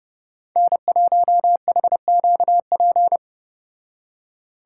Morse code, heard as a keyed tone, about 33 wpm, 700 Hz. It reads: N1HQP